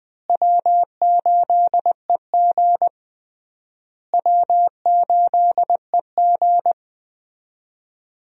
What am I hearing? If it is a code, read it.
Morse code, heard as a keyed tone, about 20 wpm, 700 Hz. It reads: W8EG W8EG
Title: USA Call Signs - 4 Characters - Repeated 3x - 20wpm